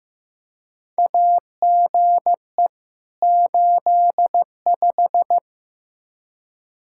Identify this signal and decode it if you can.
Morse code, heard as a keyed tone, about 15 wpm, 700 Hz. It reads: AGE 85
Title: Practice QSO - Exchange 3 - More Info - 15wpm